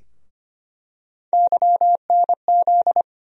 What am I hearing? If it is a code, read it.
Morse code, heard as a keyed tone, about 25 wpm, 700 Hz. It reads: YNZ
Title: USA Call Sign Postfixes - 3 Letters 25wpm